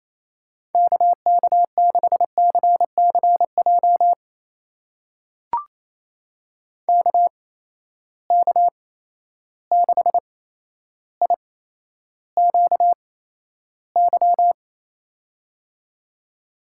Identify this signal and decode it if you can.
Morse code, heard as a keyed tone, about 28 wpm, spaced as for 12 wpm, 700 Hz. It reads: KK6CCJ KK6IQY
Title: USA Call Signs - 6 Characters - 8x Character Spacing - 28wpm